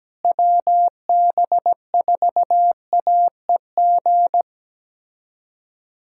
Morse code, heard as a keyed tone, about 17 wpm, 700 Hz. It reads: WB4AEG
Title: K1USN SST QSO Practice - 17wpm